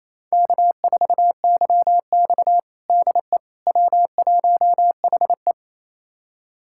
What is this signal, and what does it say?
Morse code, heard as a keyed tone, about 28 wpm, 700 Hz. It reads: K4YX DE W1HE
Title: USA Callsign DE Callsign - 4 Characters - 28wpm